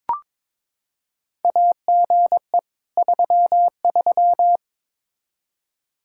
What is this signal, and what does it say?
Morse code, heard as a keyed tone, about 21 wpm, 700 Hz. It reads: AGE 33
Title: K1USN SST QSO Practice 22wpm